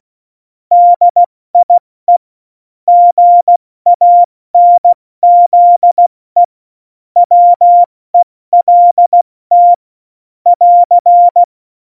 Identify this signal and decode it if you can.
Morse code, heard as a keyed tone, about 16 wpm, spaced as for 14 wpm, 700 Hz. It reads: DIE GANZE WELT <AR>
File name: Morsekurs / Witze / Witz 18. Dezember, 70 F80